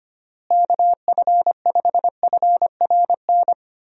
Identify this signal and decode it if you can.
Morse code, heard as a keyed tone, about 25 wpm, 700 Hz. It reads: KF5FRN
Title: USA Call Signs - 6 Characters - Repeated 2x - 25wpm